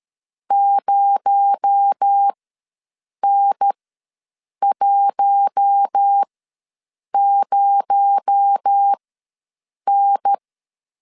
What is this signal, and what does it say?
Morse code, heard as a keyed tone, about 13 wpm, spaced as for 7 wpm, 800 Hz. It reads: 0N10N
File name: LEZIONI / 6 PKX0 / 6 PKX0 ws 12wpm 3448